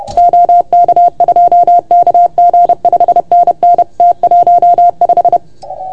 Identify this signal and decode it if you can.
Morse code, heard as a keyed tone, about 30 wpm, 690 Hz. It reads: OK2KG 5NN T15
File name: Archiv / CUC381 / OK1FGD CUC 381